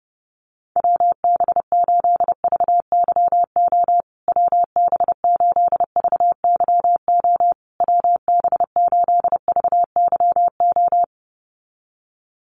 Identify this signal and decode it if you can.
Morse code, heard as a keyed tone, about 30 wpm, 700 Hz. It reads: WB8VYO WB8VYO WB8VYO
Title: USA Call Signs Repeated 3x 30wpm